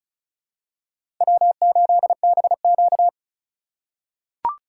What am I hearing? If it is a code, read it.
Morse code, heard as a keyed tone, about 36 wpm, 700 Hz. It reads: W8BQ
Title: USA Call Signs - 4 Characters - 3x Character Spacing - 35wpm